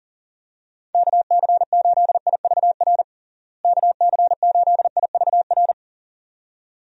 Morse code, heard as a keyed tone, about 40 wpm, 700 Hz. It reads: KC8IVR KC8IVR
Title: USA Call Signs - 6 Characters - Repeated 2x - 40wpm